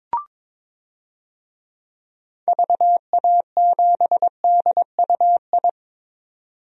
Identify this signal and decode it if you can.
Morse code, heard as a keyed tone, about 22 wpm, 700 Hz. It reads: VA7DUI
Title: DXCC-100 Call Signs - 6 Characters - 22wpm